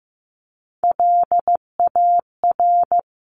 Morse code, heard as a keyed tone, about 15 wpm, 700 Hz. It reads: LAR
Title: CWT Exchange 15wpm